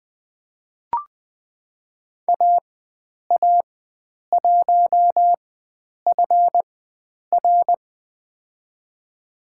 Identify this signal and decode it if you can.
Morse code, heard as a keyed tone, about 20 wpm, spaced as for 9 wpm, 700 Hz. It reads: AA1FR
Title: USA Call Signs - 4x Character Spacing - 20wpm